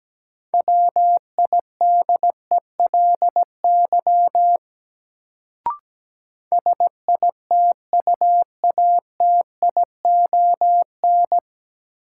Morse code, heard as a keyed tone, about 17 wpm, 700 Hz. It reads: WIDELY SITUATION
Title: Sentences from Top 1200 Words - New Word Drill Alt - 17wpm